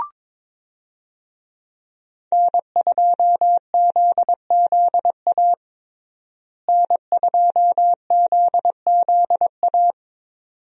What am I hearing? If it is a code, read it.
Morse code, heard as a keyed tone, about 22 wpm, 700 Hz. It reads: N2ZZA N2ZZA
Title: USA Call Signs - 5 Characters - Repeated 2x - 22wpm